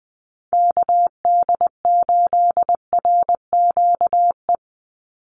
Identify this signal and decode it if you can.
Morse code, heard as a keyed tone, about 20 wpm, 700 Hz. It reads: KD8RQE
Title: Call Signs Repeated 2x - 20wpm